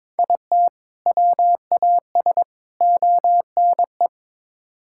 Morse code, heard as a keyed tone, about 22 wpm, 700 Hz. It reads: IT WAS ONE
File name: practice / 22wpm / Build A Phrase - 5-Word Phrases - 22wpm